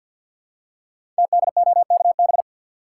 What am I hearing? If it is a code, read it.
Morse code, heard as a keyed tone, about 50 wpm, 700 Hz. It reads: TDOKB
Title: Code Groups - 5 Letters - 50wpm